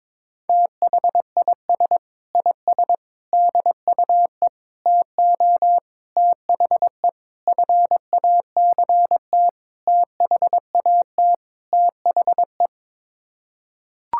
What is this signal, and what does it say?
Morse code, heard as a keyed tone, about 22 wpm, 700 Hz. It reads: THIS IS DUE TO THE FACT THAT THE
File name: practice / 22wpm / Sets of 8 Words - 22wpm